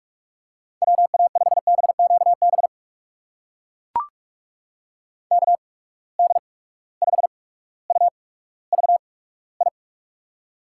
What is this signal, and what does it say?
Morse code, heard as a keyed tone, about 44 wpm, spaced as for 19 wpm, 700 Hz. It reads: WA5BQB KD5UVI
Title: USA Call Signs - 6 Characters - 8x Character Spacing - 45wpm